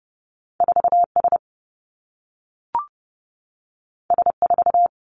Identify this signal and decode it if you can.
Morse code, heard as a keyed tone, about 30 wpm, 700 Hz. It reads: 4S S4
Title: Two Challenging Letter-Number Combinations 30wpm